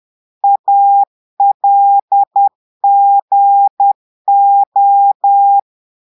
Morse code, heard as a keyed tone, about 10 wpm, 800 Hz. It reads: ALGO
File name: portugais texte en clair 2 vitesse 10 mots minutes